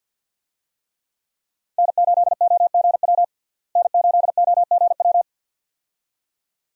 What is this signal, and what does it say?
Morse code, heard as a keyed tone, about 52 wpm, 700 Hz. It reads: N8OGW N8OGW
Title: USA Call Signs - 5 Characters - Repeated 2x - 50wpm